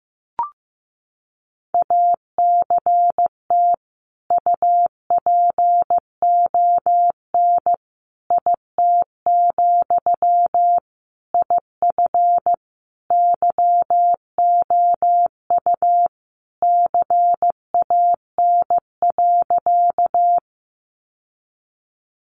Morse code, heard as a keyed tone, about 14 wpm, 700 Hz. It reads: ACT UPON IT, IF YOU CAN.
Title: Sentences from Top 1100 Words 15wpm